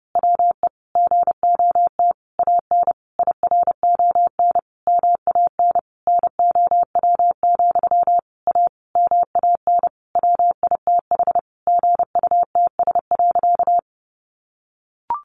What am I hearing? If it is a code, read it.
Morse code, heard as a keyed tone, about 28 wpm, 700 Hz. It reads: WE GOT AN IRON MAN NOW, A MAN WITH GUTS.
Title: A Sound Of Thunder 30wpm